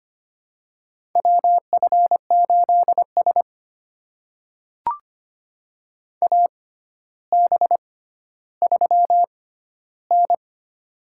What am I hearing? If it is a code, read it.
Morse code, heard as a keyed tone, about 25 wpm, spaced as for 13 wpm, 700 Hz. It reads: WF8S AB3N
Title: USA Call Signs - 4 Characters - 6x Character Spacing - 25wpm